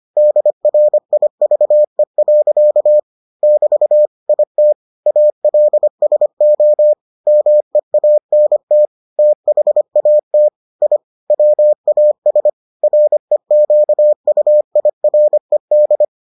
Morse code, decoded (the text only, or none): DRIVE. <BT> IT ALSO MEANT THAT I WAS REQUIRED